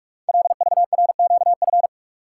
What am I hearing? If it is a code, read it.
Morse code, heard as a keyed tone, about 44 wpm, 700 Hz. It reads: LVRQF
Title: Code Groups - 5 Letters - 45wpm